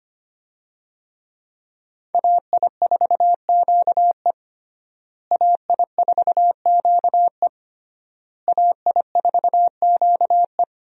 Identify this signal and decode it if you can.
Morse code, heard as a keyed tone, about 25 wpm, 700 Hz. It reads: AI4QE AI4QE AI4QE
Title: USA Call Signs - 5 Characters - Repeated 3x - 25wpm